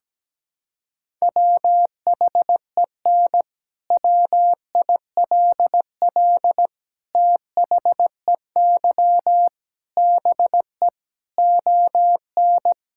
Morse code, heard as a keyed tone, about 17 wpm, 700 Hz. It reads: WHEN WILL THEY BE ON
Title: Sentences from Top 1000 Words - Review Part 6 - 17wpm